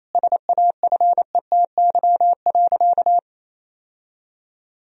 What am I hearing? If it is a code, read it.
Morse code, heard as a keyed tone, about 28 wpm, 700 Hz. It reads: SAFETY.
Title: Sentences from Top 1000 Words - Review Part 4 - 28wpm